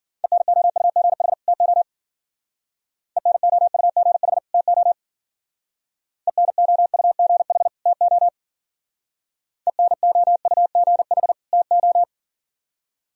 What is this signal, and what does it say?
Morse code, heard as a keyed tone, about 48 wpm, 700 Hz. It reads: ENOUGH TO ENOUGH TO ENOUGH TO ENOUGH TO
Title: Speed-Racing - Sets of 2 Words 40wpm